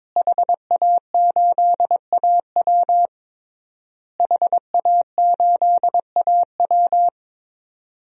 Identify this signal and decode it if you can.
Morse code, heard as a keyed tone, about 22 wpm, 700 Hz. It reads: HA8AW HA8AW
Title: DXCC-100 Call Signs - 5 Characters - Repeated 3x - 22wpm